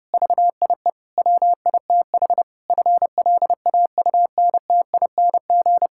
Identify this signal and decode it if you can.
Morse code, heard as a keyed tone, about 30 wpm, 700 Hz. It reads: VIE WITH FLAUNTING
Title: Pangram Challenge v2 - 30wpm